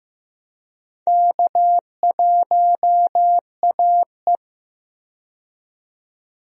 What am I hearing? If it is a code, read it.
Morse code, heard as a keyed tone, about 15 wpm, 700 Hz. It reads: K1AE
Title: USA Call Signs - 4 Characters - 15wpm